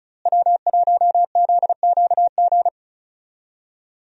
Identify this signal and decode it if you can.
Morse code, heard as a keyed tone, about 34 wpm, 700 Hz. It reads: W1ZQG
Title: USA Call Signs - 5 Characters - Repeated 3x - 35wpm